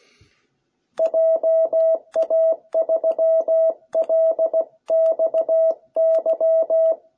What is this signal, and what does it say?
Morse code, heard as a keyed tone, about 16 wpm, 660 Hz. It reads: JA3LXY